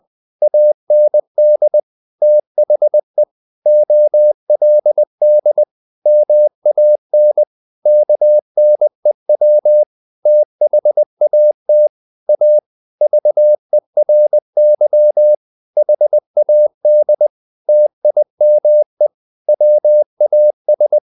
Morse code, heard as a keyed tone, about 20 wpm, 600 Hz. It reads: AND THE OLD MAN KNEW THAT A VERY HAD TIME WAS
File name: the old man and the sea part 2 20wpm